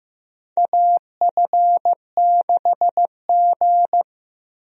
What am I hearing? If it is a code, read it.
Morse code, heard as a keyed tone, about 15 wpm, 700 Hz. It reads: AF6G
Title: USA Call Signs - 4 Characters - Repeated 2x - 15wpm